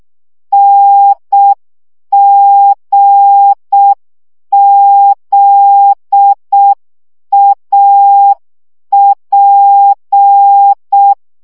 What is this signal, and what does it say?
Morse code, heard as a keyed tone, about 6 wpm, 800 Hz. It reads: NGZAP
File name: morze vizsga 2011-3 5